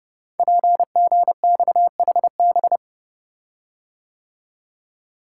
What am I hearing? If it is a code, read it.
Morse code, heard as a keyed tone, about 29 wpm, 700 Hz. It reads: PGXHB
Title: Code Groups - 5 Letters - 30wpm